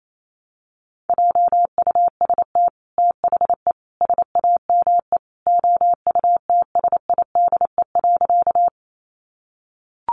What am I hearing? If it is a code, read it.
Morse code, heard as a keyed tone, about 28 wpm, 700 Hz. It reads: JUST THE SAME OUTSIDE.
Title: Sentences from Top 900 Words 28wpm